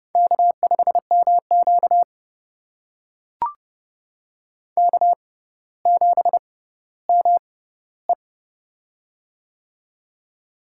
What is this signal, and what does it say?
Morse code, heard as a keyed tone, about 30 wpm, spaced as for 15 wpm, 700 Hz. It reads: K5MQ K7ME
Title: USA Call Signs - 4 Characters - 6x Character Spacing - 30wpm